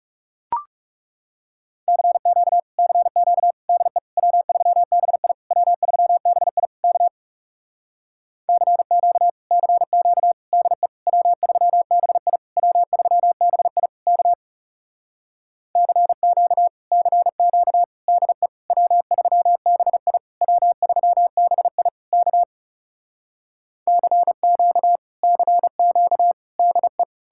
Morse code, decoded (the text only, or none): CQ CQ DE W3BI W3BI K CQ CQ DE W3BI W3BI K CQ CQ DE W3BI W3BI K CQ CQ DE